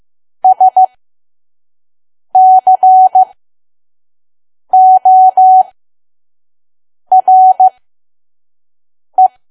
Morse code, heard as a keyed tone, about 15 wpm, spaced as for 5 wpm, 750 Hz. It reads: SCORE
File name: Archive / 5 WPM files / PM